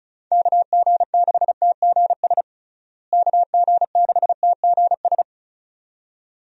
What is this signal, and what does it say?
Morse code, heard as a keyed tone, about 34 wpm, 700 Hz. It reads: KG6TGS KG6TGS
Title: USA Call Signs - 6 Characters - Repeated 2x - 35wpm